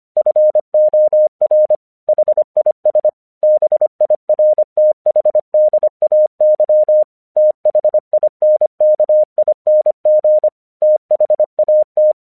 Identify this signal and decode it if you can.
Morse code, heard as a keyed tone, about 25 wpm, 610 Hz. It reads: FOR HIS BIRTHDAY THINKING THAT
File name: 1 English 25wpm / AQC14